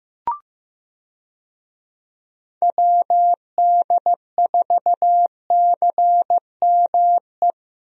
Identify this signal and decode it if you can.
Morse code, heard as a keyed tone, about 15 wpm, 700 Hz. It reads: WD4CME
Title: USA Call Signs - 6 Characters - Repeated 3x - 15wpm